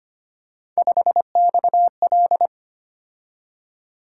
Morse code, heard as a keyed tone, about 25 wpm, 700 Hz. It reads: 5XL
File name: Three Letter-Number Combinations - 25wpm